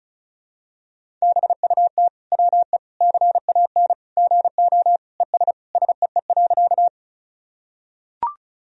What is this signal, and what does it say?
Morse code, heard as a keyed tone, about 34 wpm, 700 Hz. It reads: BUT WE CAN GO ES SEE.
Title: Sentences from Top 100 Words and Abbreviations 35wpm